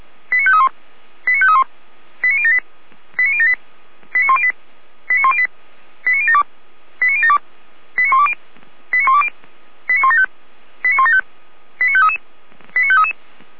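Five-tone selective call (ZVEI), heard as quick runs of five tones, tones of about 70 ms. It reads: 89531 89531 89078 89078 89108 89108 89072 89072 89120 89120 89175 89175 89533 89533
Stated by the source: ZVEI five-tone selective call 89173, received on SDR by Feuerwehr Orlamünde.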